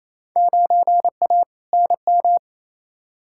Morse code, heard as a keyed tone, about 28 wpm, 700 Hz. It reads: 9A NM
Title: Field Day Exchanges Encore - 28wpm